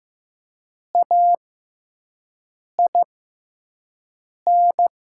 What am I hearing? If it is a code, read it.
Morse code, heard as a keyed tone, about 15 wpm, spaced as for 5 wpm, 700 Hz. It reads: AIN
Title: 6x Character Spacing - Top 300 Words - 15wpm